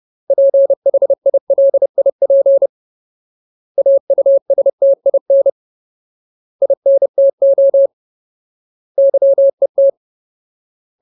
Morse code, heard as a keyed tone, about 30 wpm, 550 Hz. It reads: PHILIP AUSTIN INTO YET
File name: Ultimate Word List / 30WPM Ultimate